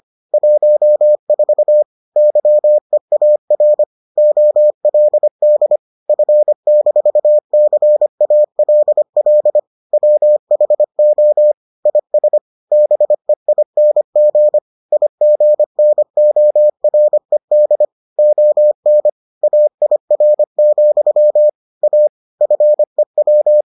Morse code, heard as a keyed tone, about 25 wpm, 600 Hz. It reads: 14 YEAR OLD F-CALL WHO IS BEING IGNORED ON AIR, A FEW